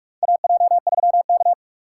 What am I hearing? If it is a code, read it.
Morse code, heard as a keyed tone, about 45 wpm, 700 Hz. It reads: AJ3K
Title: USA Call Signs - 2x Character Spacing - 45wpm